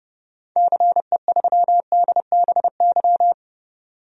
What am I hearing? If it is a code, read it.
Morse code, heard as a keyed tone, about 30 wpm, 700 Hz. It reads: CE3DBY
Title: DXCC-100 Call Signs - 6 Characters - 30wpm